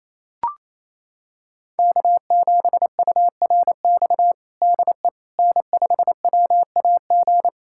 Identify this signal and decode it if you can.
Morse code, heard as a keyed tone, about 27 wpm, 700 Hz. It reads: K7URX DE N5WAG
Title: USA Callsign DE Callsign - 5 Characters - 28wpm